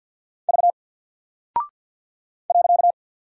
Morse code, heard as a keyed tone, about 50 wpm, 700 Hz. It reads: V .
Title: Single Character (Letter:Number:Punctuation:Prosign) Alt 50wpm